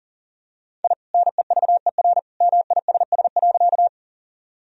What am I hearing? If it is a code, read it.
Morse code, heard as a keyed tone, about 40 wpm, 700 Hz. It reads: I NEVER MISS.
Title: Sentences from Top 1000 Words - Review Part 3 - 40wpm